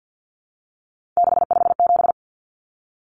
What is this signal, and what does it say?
Morse code, heard as a keyed tone, about 50 wpm, 700 Hz. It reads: 657
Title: Common Cut Numbers - 3-Digit 50wpm